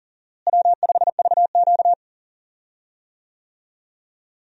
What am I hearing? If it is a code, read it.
Morse code, heard as a keyed tone, about 40 wpm, 700 Hz. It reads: W5VQ
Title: USA Call Signs - 4 Characters - Repeated 3x - 40wpm